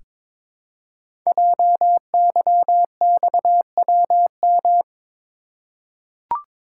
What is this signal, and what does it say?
Morse code, heard as a keyed tone, about 22 wpm, 700 Hz. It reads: JYXWM
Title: Code Groups - 5 Letters - 22wpm